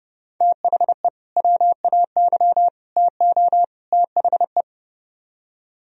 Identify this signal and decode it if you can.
Morse code, heard as a keyed tone, about 30 wpm, 700 Hz. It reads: THE WAY TO THE